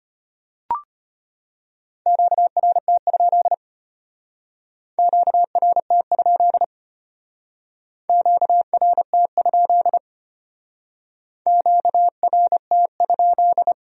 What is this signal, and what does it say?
Morse code, heard as a keyed tone, about 31 wpm, 700 Hz. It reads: QRT? QRT? QRT? QRT?
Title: Speed-Racing Q-Codes - 25wpm